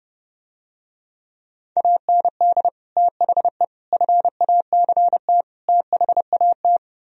Morse code, heard as a keyed tone, about 30 wpm, 700 Hz. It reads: AND THE FACT THAT